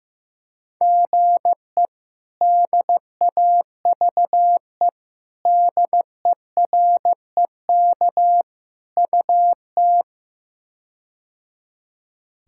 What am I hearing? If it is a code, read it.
Morse code, heard as a keyed tone, about 16 wpm, 700 Hz. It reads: GE DAVE DEREK UT
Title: K1USN SST QSO Practice - 15wpm